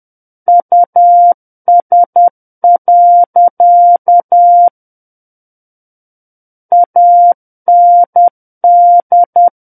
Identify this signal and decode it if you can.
Morse code, heard as a keyed tone, about 10 wpm, 700 Hz. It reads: US. AND
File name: Book One / 01-Chapter 1 - 10 WPM